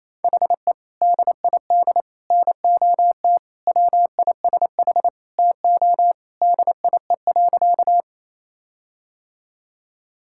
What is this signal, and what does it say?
Morse code, heard as a keyed tone, about 27 wpm, 700 Hz. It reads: HE DID NOT WISH TO DIE.
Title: Sentences from Top 1600 Words 28wpm